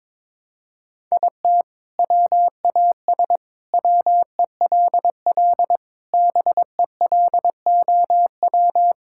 Morse code, heard as a keyed tone, about 21 wpm, 700 Hz. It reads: IT WAS WELL BELOW
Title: Sentences from Top 1000 Words - Review Part 6 - 22wpm